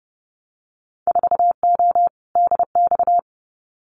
Morse code, heard as a keyed tone, about 30 wpm, 700 Hz. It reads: 4O DX
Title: Winter Field Day Exchanges Encore 30wpm